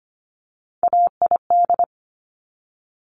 Morse code, heard as a keyed tone, about 25 wpm, 700 Hz. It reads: AID
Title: Top Three Letter Combinations in Words 25wpm